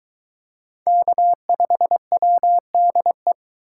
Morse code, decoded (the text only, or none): K5WDE